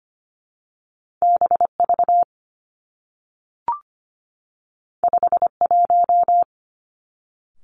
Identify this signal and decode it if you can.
Morse code, heard as a keyed tone, about 25 wpm, 700 Hz. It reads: B V 5 1
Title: Two Challenging Letter-Number Combinations 25wpm